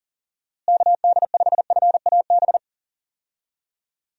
Morse code, heard as a keyed tone, about 40 wpm, 700 Hz. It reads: KD5FAB
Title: USA Call Signs - 6 Characters - Repeated 2x - 40wpm